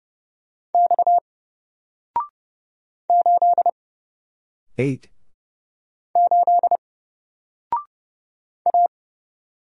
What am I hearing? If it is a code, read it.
Morse code, heard as a keyed tone, about 30 wpm, 700 Hz. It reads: X 8 8 A
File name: Single-Letter-Number-30wpm